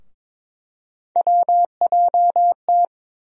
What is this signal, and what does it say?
Morse code, heard as a keyed tone, about 22 wpm, 700 Hz. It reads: WJT
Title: USA Call Sign Postfixes - 3 Letters 22wpm